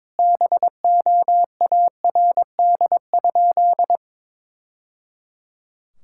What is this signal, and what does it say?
Morse code, heard as a keyed tone, about 22 wpm, 700 Hz. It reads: BOARD?
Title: Sentences from Top 1000 Words - Review Part 6 - 22wpm